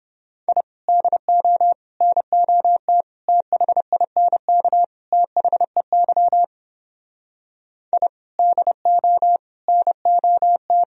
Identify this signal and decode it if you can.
Morse code, heard as a keyed tone, about 29 wpm, 700 Hz. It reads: I DO NOT THINK THEY I DO NOT